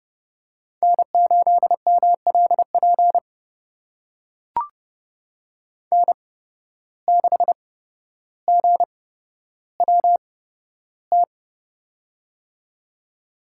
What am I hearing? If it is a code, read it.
Morse code, heard as a keyed tone, about 30 wpm, spaced as for 13 wpm, 700 Hz. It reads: N8MLP N6GWT
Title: USA Call Signs - 5 Characters - 8x Character Spacing - 30wpm